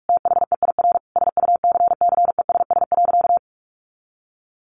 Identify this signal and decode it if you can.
Morse code, heard as a keyed tone, about 44 wpm, 700 Hz. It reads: THEIR SUCCESS.
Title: Sentences from Top 900 Words 45wpm